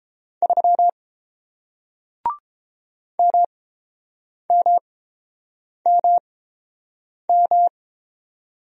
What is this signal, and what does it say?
Morse code, heard as a keyed tone, about 30 wpm, spaced as for 7 wpm, 700 Hz. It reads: 3 MMMM